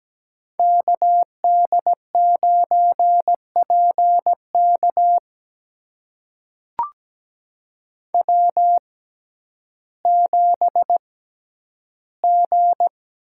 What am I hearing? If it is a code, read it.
Morse code, heard as a keyed tone, about 17 wpm, spaced as for 10 wpm, 700 Hz. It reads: KD9PK W7G
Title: USA Call Signs - 5 Characters - 6x Character Spacing - 17wpm